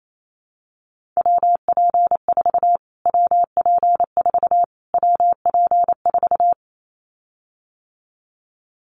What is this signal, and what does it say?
Morse code, heard as a keyed tone, about 28 wpm, 700 Hz. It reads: WP4 WP4 WP4
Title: Callsign Prefixes for Top 25 DXCC - Repeated 3x - 28wpm